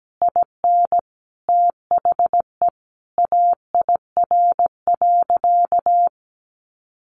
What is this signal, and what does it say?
Morse code, heard as a keyed tone, about 17 wpm, 700 Hz. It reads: IN THE AIR.
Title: Sentences from Top 600 Words 17wpm